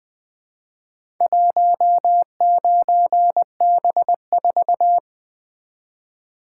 Morse code, heard as a keyed tone, about 20 wpm, 700 Hz. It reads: 19B4
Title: Four Letter-Number Combinations - 20wpm